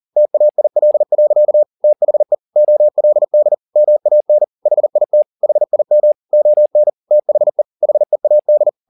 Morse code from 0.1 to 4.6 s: TAIL. THE OLD MAN

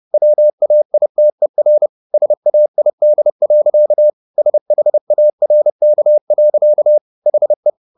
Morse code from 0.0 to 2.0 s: WAITER

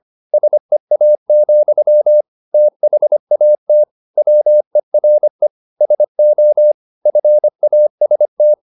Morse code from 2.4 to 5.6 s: THAT WERE

